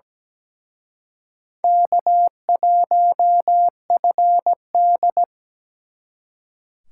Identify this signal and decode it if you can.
Morse code, heard as a keyed tone, about 20 wpm, 700 Hz. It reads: K1FD
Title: Speed Racing - USA 1x2 Call Signs 17wpm